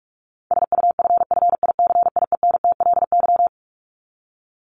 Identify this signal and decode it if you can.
Morse code, heard as a keyed tone, about 44 wpm, 700 Hz. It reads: SUFFICIENTLY
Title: LY Suffix - Top 100 Words with Base Words 45wpm